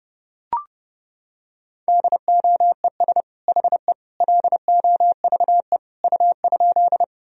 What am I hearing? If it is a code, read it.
Morse code, heard as a keyed tone, about 30 wpm, 700 Hz. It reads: DOES HE LOVE U?
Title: Sentences from Top 400 Words and Abbreviations 30wpm